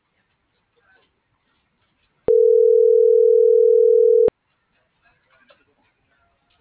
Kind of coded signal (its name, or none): telephone call progress tone